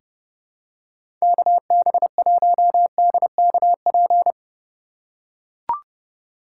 Morse code, heard as a keyed tone, about 32 wpm, 700 Hz. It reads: KB1DKP
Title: USA Call Signs - 6 Characters - 3x Character Spacing - 30wpm